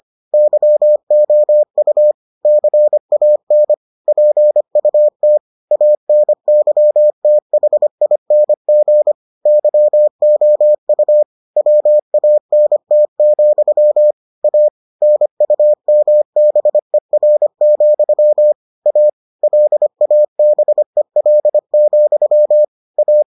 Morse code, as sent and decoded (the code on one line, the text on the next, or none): -.-- --- ..- / -.-. .- -. / .--. ..- - / .- -. -.-- - .... .. -. --. / -.-- --- ..- / .-- .- -. - --..-- / .- / -. ..- -- -... .-. --..-- / .- / .-.. .- -... .-.. --..-- / .-
YOU CAN PUT ANYTHING YOU WANT, A NUMBER, A LABEL, A